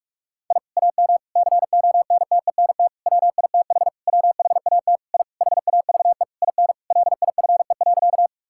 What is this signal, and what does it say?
Morse code, heard as a keyed tone, about 46 wpm, 700 Hz. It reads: I AM CONTENT WITH WHAT I HAVE IN LIFE.
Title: Sentences from Top 1000 Words - Review Part 6 - 45wpm